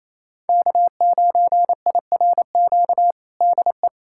Morse code, heard as a keyed tone, about 28 wpm, 690 Hz. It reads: K9IRQ DE
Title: USA Callsign DE Callsign - 5 Characters - 28wpm